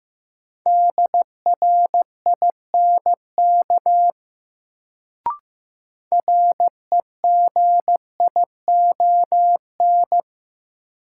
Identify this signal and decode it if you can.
Morse code, heard as a keyed tone, about 15 wpm, 700 Hz. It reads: DRINK REGION
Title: Sentences from Top 1500 Words - New Word Drill - 15wpm